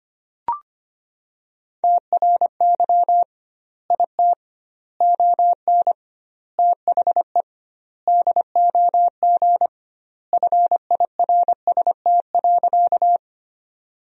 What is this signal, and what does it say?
Morse code, heard as a keyed tone, about 25 wpm, 700 Hz. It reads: TRY IT ON THE DOG FIRST.
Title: 2x Word-Spacing - Sentences from Top 400 Words - 25wpm